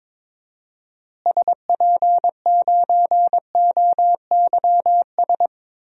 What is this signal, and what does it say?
Morse code, heard as a keyed tone, about 22 wpm, 700 Hz. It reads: SP9OYS
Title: DXCC-100 Call Signs - 6 Characters - Repeated 2x - 22wpm